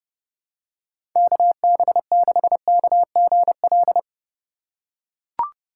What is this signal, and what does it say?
Morse code, heard as a keyed tone, about 30 wpm, 700 Hz. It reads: KB6KGL
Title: USA Call Signs - 3x Character Spacing - 30wpm